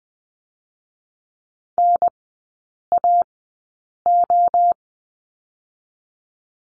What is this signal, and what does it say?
Morse code, heard as a keyed tone, about 20 wpm, 700 Hz. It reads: N A O
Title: 023 Lesson 5e - Letter o - Three Characters 20wpm